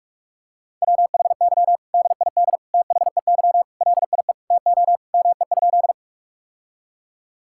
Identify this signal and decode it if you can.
Morse code, heard as a keyed tone, about 41 wpm, 700 Hz. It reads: WHY DID THEY LIE TO ME?
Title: Sentences from Top 1600 Words 45wpm